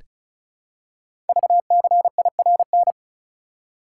Morse code, heard as a keyed tone, about 33 wpm, 700 Hz. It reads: VCIRN
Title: Code Groups - 5 Letter:Numbers - 35wpm